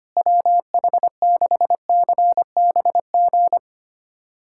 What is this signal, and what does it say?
Morse code, heard as a keyed tone, about 25 wpm, 700 Hz. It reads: WH6CBG
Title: USA Callsign DE Callsign - 6 Characters - 25wpm